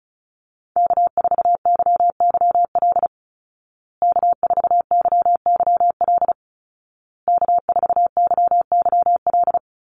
Morse code, heard as a keyed tone, about 34 wpm, 700 Hz. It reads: K4YYL K4YYL K4YYL
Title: Call Signs Repeated 2x - 35wpm